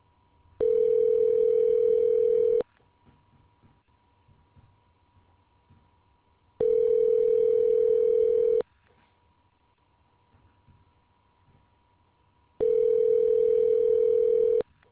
Ringback tone (North America), 440 + 480 Hz.